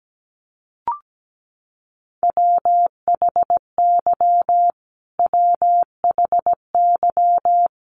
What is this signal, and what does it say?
Morse code, heard as a keyed tone, about 17 wpm, 700 Hz. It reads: WHY WHY
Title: Top 500 Words Repeated 17wpm